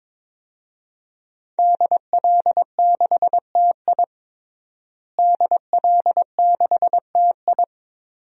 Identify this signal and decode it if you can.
Morse code, heard as a keyed tone, about 21 wpm, 700 Hz. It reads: DL6TI DL6TI
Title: DXCC-100 Call Signs - 5 Characters - Repeated 2x - 22wpm